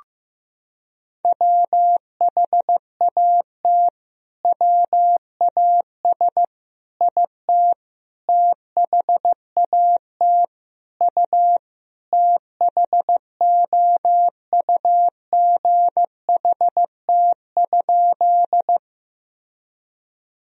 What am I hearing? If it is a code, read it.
Morse code, heard as a keyed tone, about 15 wpm, 700 Hz. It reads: WHAT WAS IT THAT U THOUGHT?
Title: Sentences from Top 300 Words and Abbreviations 15wpm